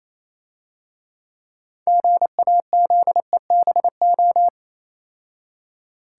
Morse code, heard as a keyed tone, about 27 wpm, 700 Hz. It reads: GAZEBO
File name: Easy Words - Speed-Racing with Hints - 22wpm